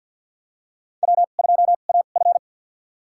Morse code, heard as a keyed tone, about 53 wpm, spaced as for 37 wpm, 700 Hz. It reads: W2AF
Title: USA Call Signs - 4 Characters - 2x Character Spacing - 50wpm